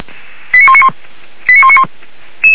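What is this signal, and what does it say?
Five-tone selective call (ZVEI), heard as quick runs of five tones, tones of about 70 ms. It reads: 89191 89191
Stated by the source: ZVEI five-tone call received by SDR.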